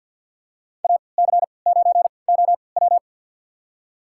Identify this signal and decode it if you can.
Morse code, heard as a keyed tone, about 53 wpm, spaced as for 29 wpm, 700 Hz. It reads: AC9OW
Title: USA Call Signs - 5 Characters - 3x Character Spacing - 50wpm